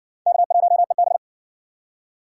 Morse code, heard as a keyed tone, about 56 wpm, 700 Hz. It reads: K1EB